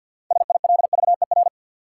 Morse code, heard as a keyed tone, about 53 wpm, 700 Hz. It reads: SILVER
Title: Six Letter Words - 50wpm